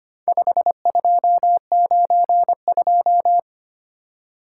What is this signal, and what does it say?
Morse code, heard as a keyed tone, about 25 wpm, 700 Hz. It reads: 5292
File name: practice / 25wpm / Four Letter-Number Combinations - 25wpm